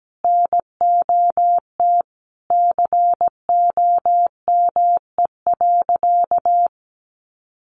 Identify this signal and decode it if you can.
Morse code, heard as a keyed tone, about 17 wpm, 700 Hz. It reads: NOT COME.
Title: Sentences from Top 1500 Words 17wpm